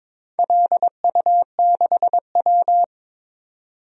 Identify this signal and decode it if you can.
Morse code, heard as a keyed tone, about 22 wpm, 700 Hz. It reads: LU6W
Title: DXCC-100 Call Signs - 4 Characters - Repeated 2x - 22wpm